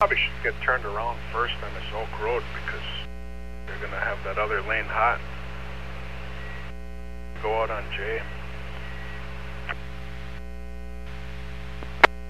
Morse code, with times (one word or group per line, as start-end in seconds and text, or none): none